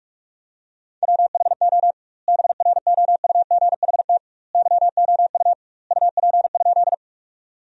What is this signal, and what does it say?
Morse code, heard as a keyed tone, about 43 wpm, 700 Hz. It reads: WHO BROUGHT YOU UP?
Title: 2x Word-Spacing - Sentences from Top 500 Words - 45wpm